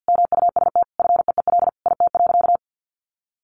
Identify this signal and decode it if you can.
Morse code, heard as a keyed tone, about 50 wpm, 700 Hz. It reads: MUST FEEL IT.
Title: Sentences from Top 400 Words 50wpm